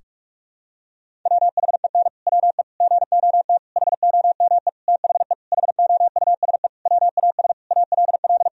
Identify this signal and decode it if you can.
Morse code, heard as a keyed tone, about 46 wpm, 700 Hz. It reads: WHEN WE GOT HOME THE HOUSE WAS ALL
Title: Sentences from Top 500 Words and Abbreviations 45wpm